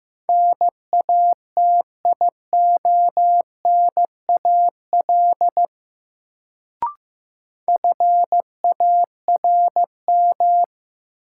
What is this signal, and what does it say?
Morse code, heard as a keyed tone, about 15 wpm, 700 Hz. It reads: NATIONAL FARM
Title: Sentences from Top 1200 Words - New Word Drill Alt - 15wpm